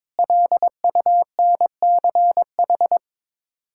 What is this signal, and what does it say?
Morse code, heard as a keyed tone, about 22 wpm, 700 Hz. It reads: LUNCH
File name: Sets of 9 Words - 22wpm